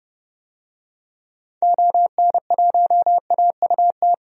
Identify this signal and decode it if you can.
Morse code, heard as a keyed tone, about 30 wpm, 700 Hz. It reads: ON1AUT
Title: DXCC-100 Call Signs - 6 Characters - 30wpm